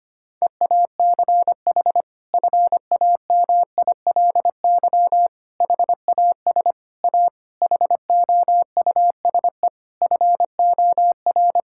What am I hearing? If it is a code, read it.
Morse code, heard as a keyed tone, about 25 wpm, 700 Hz. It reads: EACH FAMILY HAS A HOUSE FOR
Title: Sentences from Top 1000 Words - Review Part 5 - 25wpm